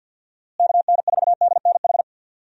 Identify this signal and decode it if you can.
Morse code, heard as a keyed tone, about 52 wpm, 700 Hz. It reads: KN4DNH
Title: USA Call Signs - 6 Characters - Repeated 3x - 50wpm